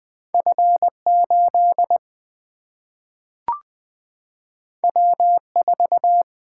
Morse code, heard as a keyed tone, about 20 wpm, 700 Hz. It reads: F8 W4
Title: Single-Letter Single-Number 20wpm